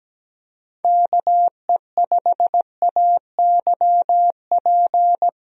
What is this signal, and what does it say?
Morse code, heard as a keyed tone, about 17 wpm, 700 Hz. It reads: KE5AYP
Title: USA Call Signs - 6 Characters - 17wpm